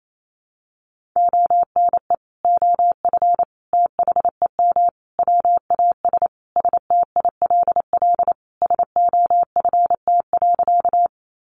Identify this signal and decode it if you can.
Morse code, heard as a keyed tone, about 28 wpm, 700 Hz. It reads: ONE OF THEM WAS STILL SOFT.
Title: Sentences from Top 1500 Words 28wpm